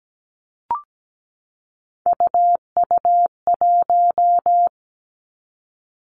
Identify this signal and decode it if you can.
Morse code, heard as a keyed tone, about 17 wpm, 700 Hz. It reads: UU1
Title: Three Challenging Letter-Number Combinations 17wpm